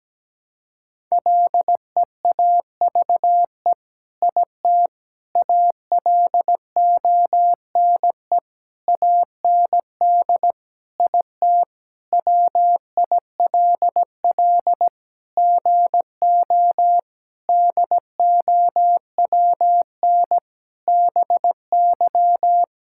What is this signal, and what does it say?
Morse code, heard as a keyed tone, about 17 wpm, 700 Hz. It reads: LEAVE IT ALONE AND IT WILL GO DOWN BY
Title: Sentences from Top 1000 Words - Review Part 5 - 17wpm